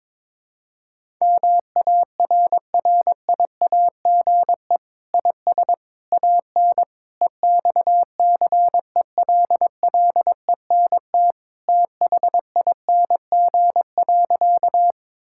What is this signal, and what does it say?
Morse code, heard as a keyed tone, about 22 wpm, 700 Hz. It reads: MARRIAGE IS AN EXCELLENT THING.
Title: Sentences from Top 1000 Words - Review Part 5 - 22wpm